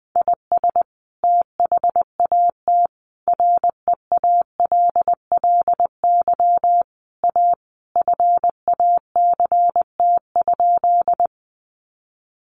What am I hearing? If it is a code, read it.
Morse code, heard as a keyed tone, about 20 wpm, 700 Hz. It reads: IS THAT REALLY A FACT?
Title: Sentences from Top 600 Words 20wpm